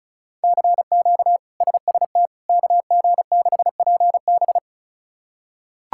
Morse code, heard as a keyed tone, about 36 wpm, 700 Hz. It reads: CQ SST KG6PB
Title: K1USN SST QSO Practice - 35wpm